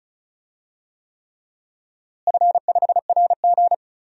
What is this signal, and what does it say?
Morse code, heard as a keyed tone, about 36 wpm, 700 Hz. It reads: F5RG
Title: DXCC-100 Call Signs - 4 Characters - Repeated 3x - 35wpm